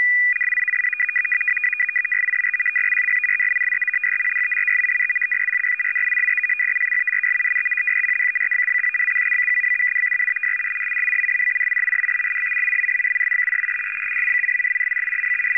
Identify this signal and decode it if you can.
Two-tone teleprinter signal, about 450 Hz shift.
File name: POL- -1000utc-fsk100-500